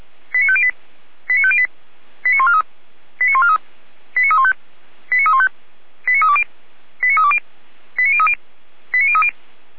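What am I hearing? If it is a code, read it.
Five-tone selective call (ZVEI), heard as quick runs of five tones, tones of about 70 ms. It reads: 89508 89508 89143 89143 89315 89315 89320 89320 89030 89030
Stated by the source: ZVEI five-tone selective call 89140 received by SDR, fire brigade Orlamünde (feuerwehr-orlamuende).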